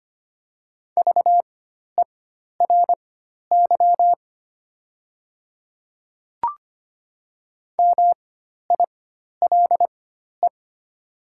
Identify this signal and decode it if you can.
Morse code, heard as a keyed tone, about 25 wpm, spaced as for 12 wpm, 700 Hz. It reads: VERY MILE
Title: 4x Character Spacing - Top 500 Words - 25wpm